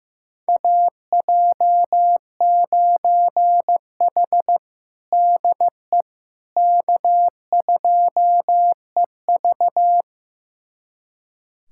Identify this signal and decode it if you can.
Morse code, heard as a keyed tone, about 15 wpm, 700 Hz. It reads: AJ9H DE K2EV
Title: USA Callsign DE Callsign - 4 Characters - 15wpm